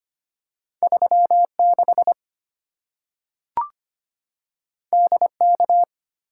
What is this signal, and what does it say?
Morse code, heard as a keyed tone, about 25 wpm, 700 Hz. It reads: 36 DK
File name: Two Letter-Number Combinations - 25wpm